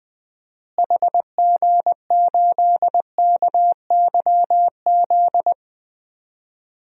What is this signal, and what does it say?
Morse code, heard as a keyed tone, about 20 wpm, 700 Hz. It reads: HG8KYZ
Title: DXCC-100 Call Signs - 6 Characters - Repeated 2x - 20wpm